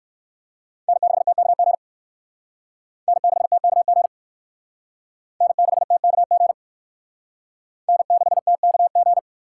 Morse code, heard as a keyed tone, about 58 wpm, 700 Hz. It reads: N6TKG N6TKG N6TKG N6TKG